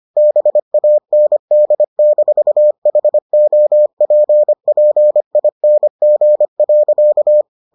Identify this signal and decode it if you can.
Morse code, heard as a keyed tone, about 25 wpm, 600 Hz. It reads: BAND-HOPPING.